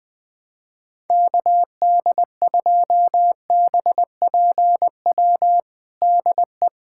Morse code, heard as a keyed tone, about 20 wpm, 700 Hz. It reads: KD2BPW DE
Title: USA Callsign DE Callsign 20wpm